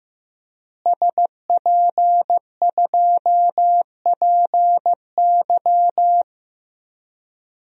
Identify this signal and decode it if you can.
Morse code, heard as a keyed tone, about 15 wpm, 700 Hz. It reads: SP2PY
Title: DXCC-100 Call Signs - 5 Characters - Repeated 2x - 15wpm